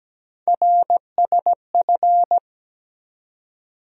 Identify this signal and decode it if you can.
Morse code, heard as a keyed tone, about 17 wpm, 700 Hz. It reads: RSF
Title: USA Call Sign Postfixes - 3 Letters 17wpm